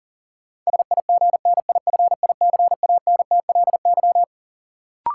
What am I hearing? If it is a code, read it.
Morse code, heard as a keyed tone, about 40 wpm, 700 Hz. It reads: SIGNIFICANTLY
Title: Sentences from Top 1500 Words - New Word Drill - 40wpm